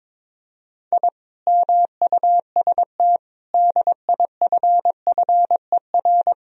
Morse code, heard as a keyed tone, about 23 wpm, 700 Hz. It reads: I MUST DIFFER
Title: Sentences from Top 1000 Words - Review Part 1 - 22wpm